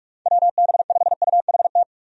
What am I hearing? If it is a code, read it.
Morse code, heard as a keyed tone, about 46 wpm, 700 Hz. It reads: WB5UHT
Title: USA Call Signs - 6 Characters - 45wpm